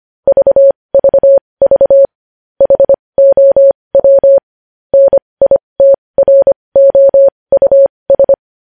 Morse code, heard as a keyed tone, about 25 wpm, spaced as for 20 wpm, 550 Hz. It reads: VVV HOW NITROUS
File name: podcast / News headlines-1x-25x20